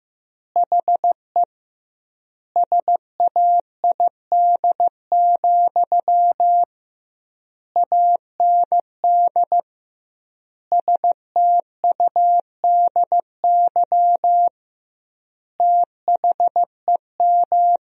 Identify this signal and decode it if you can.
Morse code, heard as a keyed tone, about 16 wpm, 700 Hz. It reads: HE SAID, AND STUDY THEM
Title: 2x Word-Spacing - Sentences from Top 300 Words - 15wpm